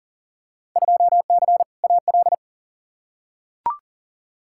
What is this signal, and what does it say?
Morse code, heard as a keyed tone, about 40 wpm, 700 Hz. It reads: 2C AL
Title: Field Day Exchanges Encore - 40wpm